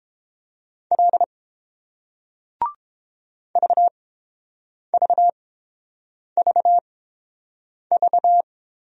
Morse code, heard as a keyed tone, about 29 wpm, spaced as for 7 wpm, 700 Hz. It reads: L VVVV